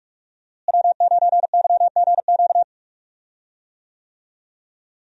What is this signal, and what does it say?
Morse code, heard as a keyed tone, about 46 wpm, 700 Hz. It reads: W9YGQ
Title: USA Call Signs - 5 Characters - Repeated 3x - 45wpm